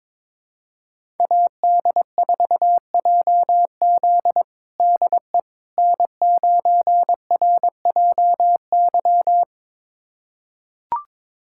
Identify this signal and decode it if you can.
Morse code, heard as a keyed tone, about 22 wpm, 700 Hz. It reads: AD4JZ DE N9RJY
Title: USA Callsign DE Callsign - 5 Characters - 22wpm